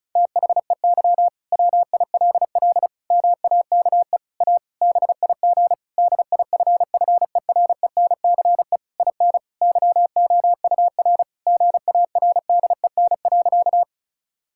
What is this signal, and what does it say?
Morse code, heard as a keyed tone, about 36 wpm, 700 Hz. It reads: THEY WILL MAKE A BIG DIFFERENCE IN YOUR GARDEN.
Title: Sentences from Top 1000 Words - Review Part 5 - 35wpm